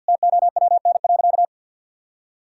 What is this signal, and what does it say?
Morse code, heard as a keyed tone, about 50 wpm, 700 Hz. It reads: TOWN.
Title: Sentences from Top 1000 Words - Review Part 6 - 50wpm